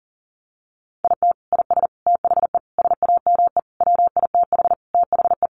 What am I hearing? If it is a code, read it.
Morse code, heard as a keyed tone, about 40 wpm, 700 Hz. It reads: IT IS THE SAME WITH THE
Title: Sentences from Top 1100 Words 40wpm